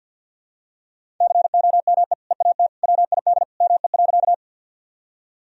Morse code, heard as a keyed tone, about 52 wpm, 700 Hz. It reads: COME EAT WID ME.
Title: Sentences from Top 300 Words and Abbreviations 50wpm